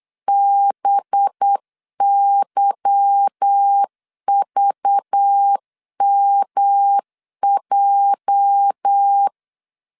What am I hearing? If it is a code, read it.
Morse code, heard as a keyed tone, about 8 wpm, 800 Hz. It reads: BYVMJ
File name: LEZIONI / 8 YJ45 / 8 YJ4 8wpm 3246